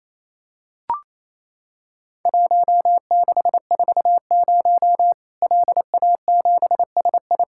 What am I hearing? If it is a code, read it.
Morse code, heard as a keyed tone, about 28 wpm, 700 Hz. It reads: 1640 LA7SI
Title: Worked All Europe DX Contest - QTC Exchange - 28wpm